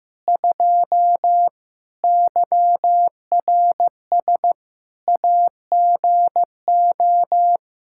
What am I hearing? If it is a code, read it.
Morse code, heard as a keyed tone, about 15 wpm, 700 Hz. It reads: 2 YRS AGO
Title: K1USN SST QSO Practice 15wpm